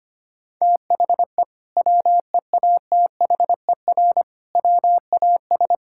Morse code, heard as a keyed tone, about 25 wpm, 700 Hz. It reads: THE WEATHER WAS
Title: Sentences from Top 1000 Words - Review Part 5 - 25wpm